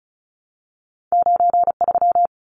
Morse code, heard as a keyed tone, about 36 wpm, 700 Hz. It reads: 93
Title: Numbers - 2-Digit 35wpm